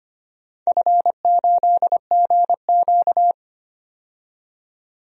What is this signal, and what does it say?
Morse code, heard as a keyed tone, about 25 wpm, 700 Hz. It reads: F8GQ
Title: DXCC-100 Call Signs - 4 Characters - 25wpm